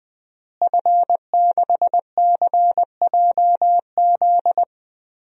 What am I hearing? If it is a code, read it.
Morse code, heard as a keyed tone, about 20 wpm, 700 Hz. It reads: F6CJZ
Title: DXCC-100 Call Signs - 5 Characters - Repeated 3x - 20wpm